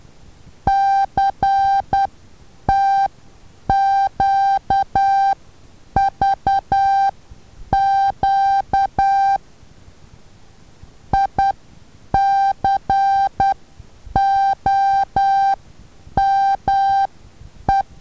Morse code, heard as a keyed tone, about 10 wpm, spaced as for 8 wpm, 790 Hz. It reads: CTQVQ ICOME
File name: Taviro HA6VA B